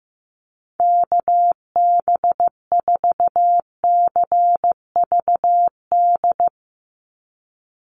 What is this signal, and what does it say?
Morse code, heard as a keyed tone, about 15 wpm, 700 Hz. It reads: KB4CVD
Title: USA Call Signs 15wpm